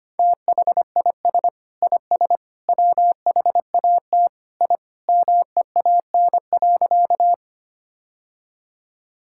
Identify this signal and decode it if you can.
Morse code, heard as a keyed tone, about 25 wpm, 700 Hz. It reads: THIS IS WHAT I MEAN.
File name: practice / 25wpm / Sentences with 3-Word Phrases - 25wpm